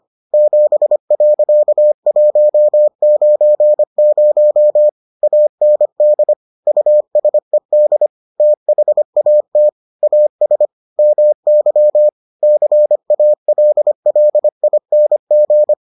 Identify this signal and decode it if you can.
Morse code, heard as a keyed tone, about 25 wpm, 600 Hz. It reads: 7.190 AND USED THAT AS MY CALLING